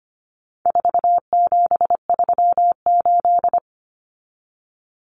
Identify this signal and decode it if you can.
Morse code, heard as a keyed tone, about 25 wpm, 700 Hz. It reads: 4738
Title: Numbers - 4-Digit 25wpm